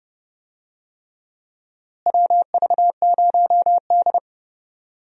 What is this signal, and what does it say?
Morse code, heard as a keyed tone, about 30 wpm, 700 Hz. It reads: WV0D
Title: USA Call Signs - 4 Characters - Repeated 3x - 30wpm